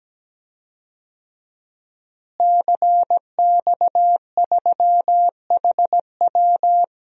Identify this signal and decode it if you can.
Morse code, heard as a keyed tone, about 17 wpm, 700 Hz. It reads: CX3HW
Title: DXCC-100 Call Signs - 5 Characters - Repeated 3x - 17wpm